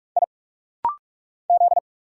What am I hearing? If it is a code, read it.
Morse code, heard as a keyed tone, about 46 wpm, 700 Hz. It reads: I Z
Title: Single Letter-Number Alt - Rapid-Fire 45wpm